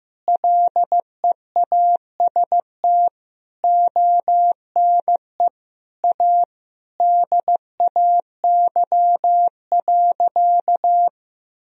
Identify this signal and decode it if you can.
Morse code, heard as a keyed tone, about 15 wpm, 700 Hz. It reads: LEAST ONE A DAY.
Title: Sentences from Top 1000 Words - Review Part 3 - 15wpm